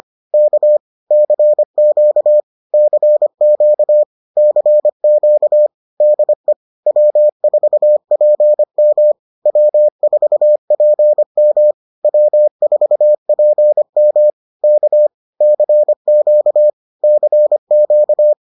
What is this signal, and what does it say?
Morse code, heard as a keyed tone, about 25 wpm, 600 Hz. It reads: K CQ CQ CQ DE W4PM W4PM W4PM K CQ CQ